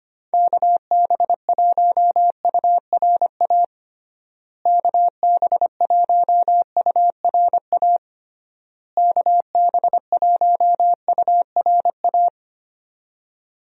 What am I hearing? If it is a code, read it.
Morse code, heard as a keyed tone, about 25 wpm, 700 Hz. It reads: KB1URA KB1URA KB1URA
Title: USA Call Signs - 6 Characters - Repeated 3x - 25wpm